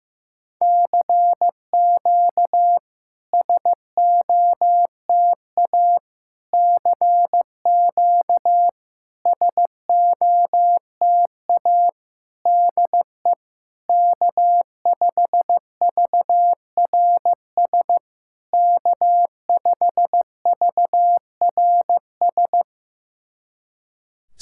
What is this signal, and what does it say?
Morse code, heard as a keyed tone, about 15 wpm, 690 Hz. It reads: CQ SOTA CQ SOTA DE K5VRS K5VRS
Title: SOTA QSO Practice 15wpm